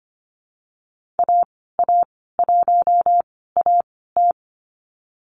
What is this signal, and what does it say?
Morse code, heard as a keyed tone, about 25 wpm, spaced as for 16 wpm, 700 Hz. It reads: AA1AT
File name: Lesson 15g - Number 5 - Call Signs 25wpm